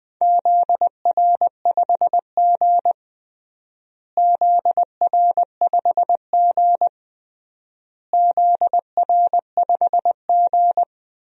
Morse code, heard as a keyed tone, about 20 wpm, 700 Hz. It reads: ZR5G ZR5G ZR5G
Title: DXCC-100 Call Signs - 4 Characters - Repeated 3x - 20wpm